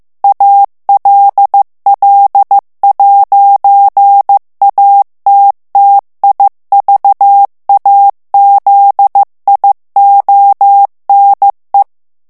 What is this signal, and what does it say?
Morse code, heard as a keyed tone, about 15 wpm, 800 Hz. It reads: ALL'ATTIVAZIONE
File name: ITALIEN texte clair 1 vitesse 14 mots minute